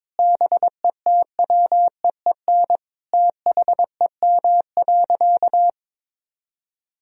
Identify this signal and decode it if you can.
Morse code, heard as a keyed tone, about 22 wpm, 700 Hz. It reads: BETWEEN THEM.
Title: Sentences from Top 1000 Words - Review Part 1 - 22wpm